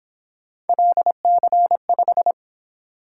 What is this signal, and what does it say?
Morse code, heard as a keyed tone, about 26 wpm, 700 Hz. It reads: LC5
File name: Speed-Racing - Three Letter-Number Combinations - 22wpm